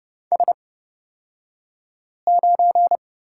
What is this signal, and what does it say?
Morse code, heard as a keyed tone, about 29 wpm, spaced as for 5 wpm, 700 Hz. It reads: S9
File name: Lesson 37d - Letter x - Two Characters 30wpm